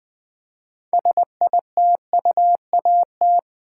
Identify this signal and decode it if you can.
Morse code, heard as a keyed tone, about 20 wpm, 700 Hz. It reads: SITUAT
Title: Build A Word - 3 Letters At A Time - 9-Letter Words - 20wpm